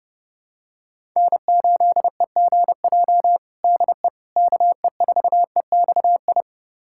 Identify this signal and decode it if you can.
Morse code, heard as a keyed tone, about 32 wpm, 700 Hz. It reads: N8EGJ DE KE4EXI
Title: USA Callsign DE Callsign 30wpm